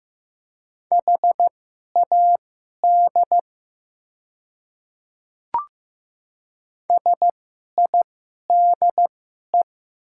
Morse code, heard as a keyed tone, about 15 wpm, spaced as for 11 wpm, 690 Hz. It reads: HAD SIDE
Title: 2x Character Spacing - Top 100 Words - 15wpm